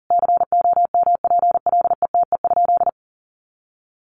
Morse code, heard as a keyed tone, about 37 wpm, 700 Hz. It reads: COMPLETE?
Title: Sentences from Top 1500 Words 40wpm